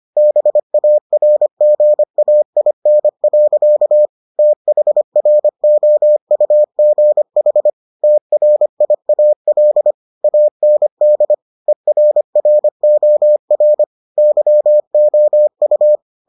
Morse code, heard as a keyed tone, about 25 wpm, 600 Hz. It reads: BARGAIN. THROUGH TRIAL AND ERROR YOU